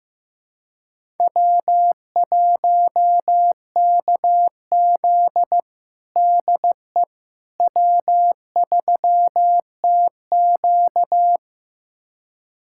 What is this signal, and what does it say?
Morse code, heard as a keyed tone, about 15 wpm, 700 Hz. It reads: W1KZ DE W3TQ
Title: USA Callsign DE Callsign - 4 Characters - 15wpm